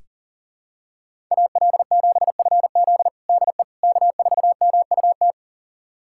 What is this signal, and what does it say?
Morse code, heard as a keyed tone, about 40 wpm, 700 Hz. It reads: AL7FZ DE K4MUT
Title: USA Callsign DE Callsign - 5 Characters - 40wpm